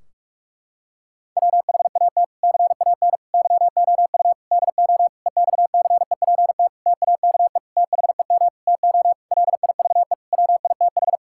Morse code, heard as a keyed tone, about 46 wpm, 700 Hz. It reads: WHAT CAN YOU DO EXCEPT TAKE THEM TO LIVE WITH
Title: Sentences from Top 1600 Words 45wpm